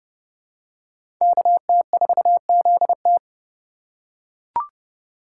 Morse code, heard as a keyed tone, about 31 wpm, 700 Hz. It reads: KT4ZT
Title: USA Call Signs - 5 Characters - 3x Character Spacing - 30wpm